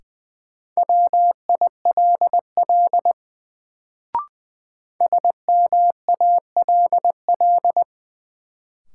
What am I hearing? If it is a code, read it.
Morse code, heard as a keyed tone, about 21 wpm, 700 Hz. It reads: WILL SMALL
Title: Sets of 2 Words - New Word Drill - 20wpm